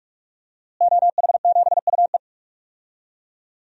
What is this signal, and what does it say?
Morse code, heard as a keyed tone, about 41 wpm, 700 Hz. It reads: OH7UE
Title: DXCC-100 Call Signs - 5 Characters - Repeated 3x - 45wpm